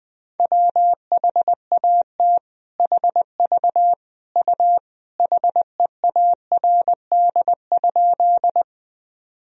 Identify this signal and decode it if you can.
Morse code, heard as a keyed tone, about 19 wpm, 700 Hz. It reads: WHAT HV U HEARD?
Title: Sentences from Top 400 Words and Abbreviations 20wpm